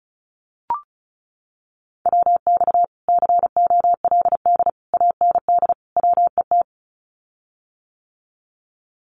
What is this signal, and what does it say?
Morse code, heard as a keyed tone, about 34 wpm, 700 Hz. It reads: WX COLD AND WET
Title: Common QSO Phrases Encore 35wpm